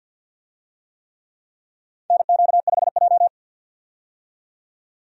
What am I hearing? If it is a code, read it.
Morse code, heard as a keyed tone, about 46 wpm, 700 Hz. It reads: NQ5J
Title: USA Call Signs - 4 Characters - 50wpm